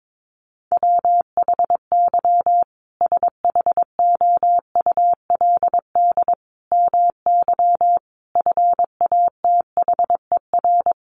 Morse code, heard as a keyed tone, about 22 wpm, 700 Hz. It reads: WHY SHOULD MY FATHER
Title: Sentences from Top 500 Words 22wpm